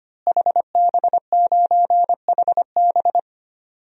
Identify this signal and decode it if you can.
Morse code, heard as a keyed tone, about 25 wpm, 700 Hz. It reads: HB9HB
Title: DXCC-100 Call Signs - 5 Characters - Repeated 2x - 25wpm